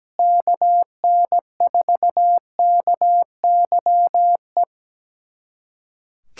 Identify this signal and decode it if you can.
Morse code, heard as a keyed tone, about 17 wpm, 700 Hz. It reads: KN4KYE
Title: USA Callsign DE Callsign 17wpm